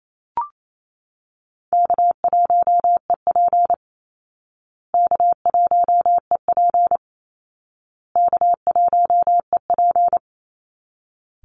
Morse code, heard as a keyed tone, about 28 wpm, 700 Hz. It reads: K1EP K1EP K1EP
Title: Easy Call Signs Repeated 3x - 28wpm